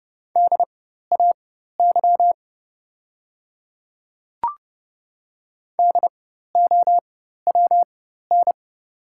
Morse code, heard as a keyed tone, about 30 wpm, spaced as for 14 wpm, 690 Hz. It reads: DAY DOWN
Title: 4x Character Spacing - Top 100 Words - 30wpm